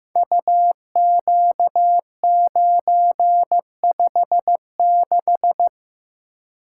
Morse code, heard as a keyed tone, about 15 wpm, 700 Hz. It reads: UQ956
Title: Code Groups - 5 Letter:Numbers - 15wpm